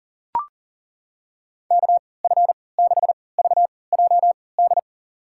Morse code, heard as a keyed tone, about 40 wpm, spaced as for 23 wpm, 700 Hz. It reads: KF6VJD
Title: USA Call Signs - 6 Characters - 3x Character Spacing - 40wpm